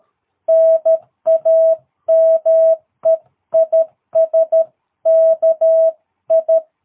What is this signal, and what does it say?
Morse code, heard as a keyed tone, about 13 wpm, 660 Hz. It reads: NAME IS KI
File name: jh8cbh / 2022cw